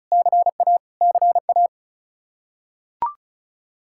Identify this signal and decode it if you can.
Morse code, heard as a keyed tone, about 36 wpm, 700 Hz. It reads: CA CA
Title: Practice QSO - Exchange 1 - Beginning - 35wpm